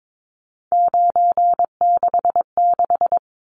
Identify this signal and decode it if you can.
Morse code, heard as a keyed tone, about 22 wpm, 700 Hz. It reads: 966
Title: Numbers - 3-Digit 22wpm